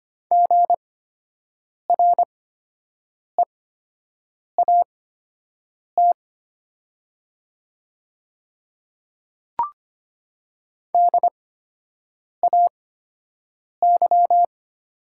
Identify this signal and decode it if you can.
Morse code, heard as a keyed tone, about 25 wpm, spaced as for 7 wpm, 700 Hz. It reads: GREAT DAY
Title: 8x Character Spacing - Top 200 Words - 25wpm